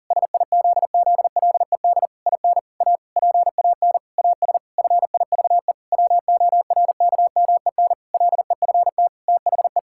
Morse code, heard as a keyed tone, about 40 wpm, 700 Hz. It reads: SIZZLED IN A PAN AS FIVE WORKMEN LEFT THE
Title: Pangram Challenge v2 - 40wpm